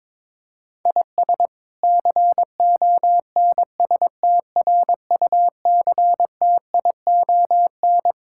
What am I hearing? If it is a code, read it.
Morse code, heard as a keyed tone, about 22 wpm, 700 Hz. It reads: IS CONSTRUCTION
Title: Sentences from Top 1600 Words 22wpm